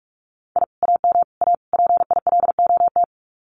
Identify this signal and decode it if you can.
Morse code, heard as a keyed tone, about 44 wpm, 700 Hz. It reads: I AM A PILOT
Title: Common QSO Phrases Encore 45wpm